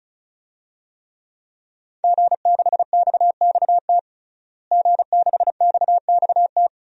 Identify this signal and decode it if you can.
Morse code, heard as a keyed tone, about 36 wpm, 700 Hz. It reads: G6XXT G6XXT
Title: DXCC-100 Call Signs - 5 Characters - Repeated 2x - 35wpm